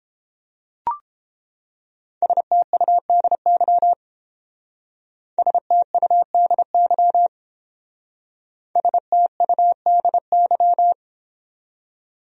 Morse code, heard as a keyed tone, about 30 wpm, 700 Hz. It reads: STUDY STUDY STUDY